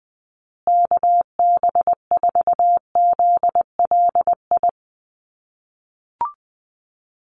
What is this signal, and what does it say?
Morse code, heard as a keyed tone, about 20 wpm, 700 Hz. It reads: KB4ZLI
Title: USA Call Signs Repeated 3x 20wpm